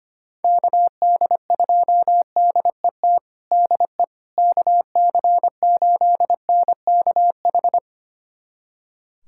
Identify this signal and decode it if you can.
Morse code, heard as a keyed tone, about 25 wpm, 700 Hz. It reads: KD2DET DE KC8NKH
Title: USA Callsign DE Callsign - 6 Characters - 25wpm